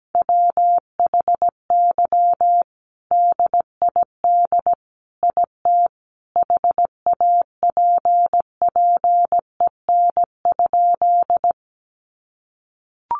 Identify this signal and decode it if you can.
Morse code, heard as a keyed tone, about 17 wpm, 700 Hz. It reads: WHY DID IT HAPPEN?
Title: Sentences from Top 400 Words 17wpm